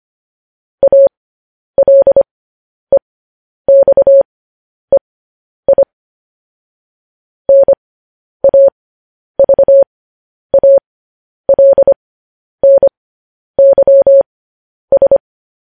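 Morse code, heard as a keyed tone, about 25 wpm, spaced as for 10 wpm, 550 Hz. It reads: ALEXEI NAVALNYS
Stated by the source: Morse code news headlines practice transmission.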